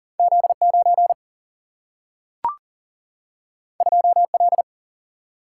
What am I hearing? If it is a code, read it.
Morse code, heard as a keyed tone, about 40 wpm, 700 Hz. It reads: Z9 2L
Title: Single-Letter Single-Number 40wpm